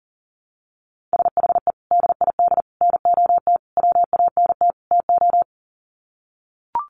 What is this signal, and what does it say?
Morse code, heard as a keyed tone, about 40 wpm, 700 Hz. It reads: SHE DID NOT WANT TO
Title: Sets of 5 Words 40wpm